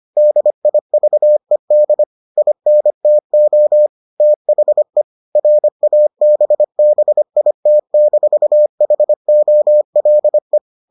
Morse code, heard as a keyed tone, about 25 wpm, 600 Hz. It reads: DIVED INTO THE RABBIT-HOLE